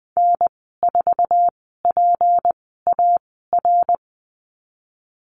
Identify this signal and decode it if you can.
Morse code, heard as a keyed tone, about 20 wpm, spaced as for 14 wpm, 700 Hz. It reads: N4PAR
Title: 193 Lesson 31g - Letter v - Call Signs 20wpm